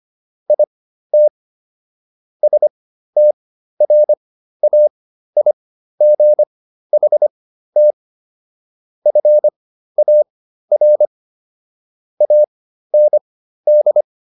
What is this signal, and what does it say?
Morse code, heard as a keyed tone, about 25 wpm, spaced as for 13 wpm, 600 Hz. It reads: IT STRAIGHT FAR AND